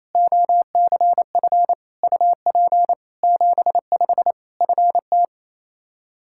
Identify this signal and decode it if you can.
Morse code, heard as a keyed tone, about 27 wpm, 700 Hz. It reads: OCF UP 75 FT
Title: K1USN SST QSO Practice 28wpm